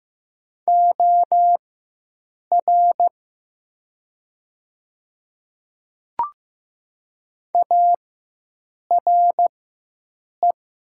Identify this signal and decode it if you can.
Morse code, heard as a keyed tone, about 15 wpm, spaced as for 7 wpm, 700 Hz. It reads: OR ARE
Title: 4x Character Spacing - Top 100 Words - 15wpm